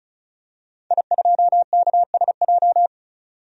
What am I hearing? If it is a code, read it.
Morse code, heard as a keyed tone, about 34 wpm, 700 Hz. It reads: I2KSJ
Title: DXCC-100 Call Signs - 5 Characters - 35wpm